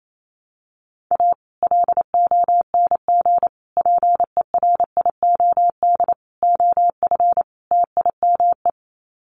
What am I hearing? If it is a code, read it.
Morse code, heard as a keyed tone, about 28 wpm, 700 Hz. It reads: A LONG PERIOD OF TIME
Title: Sets of 5 Words 28wpm